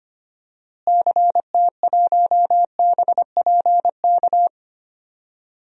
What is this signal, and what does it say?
Morse code, heard as a keyed tone, about 25 wpm, 700 Hz. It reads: CT1BPK
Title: DXCC-100 Call Signs - 6 Characters - Repeated 3x - 25wpm